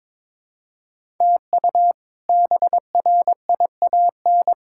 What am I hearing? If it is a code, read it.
Morse code, heard as a keyed tone, about 22 wpm, 700 Hz. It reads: TU BRIAN
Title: K1USN SST QSO Practice - 22wpm